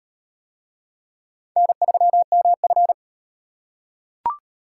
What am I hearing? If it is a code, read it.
Morse code, heard as a keyed tone, about 38 wpm, 700 Hz. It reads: N3MF